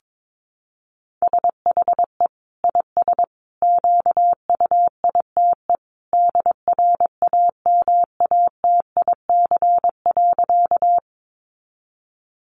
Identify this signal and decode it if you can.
Morse code, heard as a keyed tone, about 22 wpm, 700 Hz. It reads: SHE IS QUITE DRAMATIC.
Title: Sentences from Top 1300 Words 22wpm